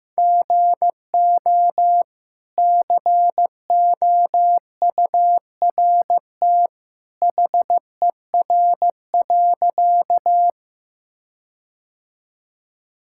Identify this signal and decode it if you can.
Morse code, heard as a keyed tone, about 15 wpm, 700 Hz. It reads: GO COURT HER.
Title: Sentences from Top 1600 Words 15wpm